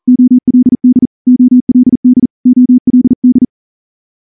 Morse code, heard as a keyed tone, about 38 wpm, 250 Hz. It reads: OLD OLD OLD
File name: morse-training-40WPM